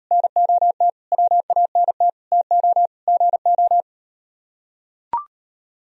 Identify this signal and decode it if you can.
Morse code, heard as a keyed tone, about 38 wpm, 700 Hz. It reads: NOT WANT TO GO